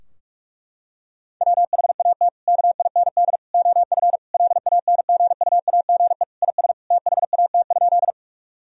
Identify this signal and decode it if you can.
Morse code, heard as a keyed tone, about 41 wpm, 700 Hz. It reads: WHAT KIND OF LANGUAGE IS THAT?
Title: Sentences from Top 1000 Words - Review Part 3 - 45wpm